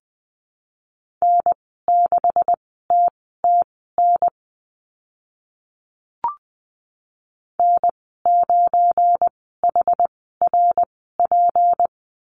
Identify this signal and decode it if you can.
Morse code, heard as a keyed tone, about 20 wpm, spaced as for 15 wpm, 700 Hz. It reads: N6TTN N9HRP
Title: 186 Lesson 30g - Letter b - Call Signs 20wpm